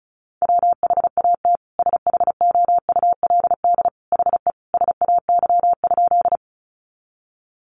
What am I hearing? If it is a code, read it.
Morse code, heard as a keyed tone, about 34 wpm, 700 Hz. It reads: WHAT SHOULD HE SAY?
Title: Sentences from Top 300 Words 35wpm